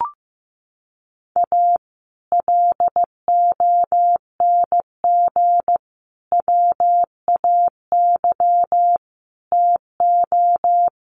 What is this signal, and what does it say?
Morse code, heard as a keyed tone, about 15 wpm, 700 Hz. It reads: A LONG WAY TO
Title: Sets of 5 Words 15wpm